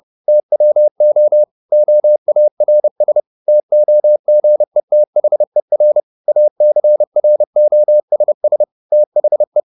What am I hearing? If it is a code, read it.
Morse code, heard as a keyed tone, about 30 wpm, 600 Hz. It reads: TWO OARS TOGETHER ACROSS THE